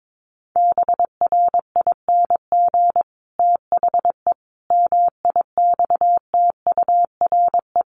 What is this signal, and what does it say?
Morse code, heard as a keyed tone, about 22 wpm, 700 Hz. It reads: BRING THE MIXTURE
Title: Sentences from Top 1500 Words 22wpm